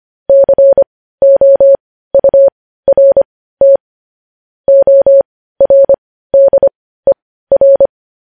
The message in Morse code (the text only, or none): COURT ORDER